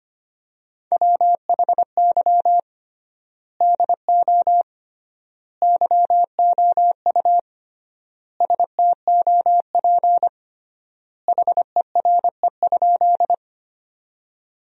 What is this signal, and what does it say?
Morse code, heard as a keyed tone, about 25 wpm, 700 Hz. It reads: WHY DO YOU STOP HERE?
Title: 3x Word-Spacing - Sentences from Top 300 Words - 25wpm